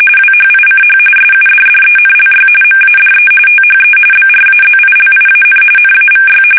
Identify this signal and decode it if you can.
RTTY, about 45.45 baud. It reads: WHICH SUDDENLY MADE THE BRAZILIAN NAVY
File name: fsk850 / battleships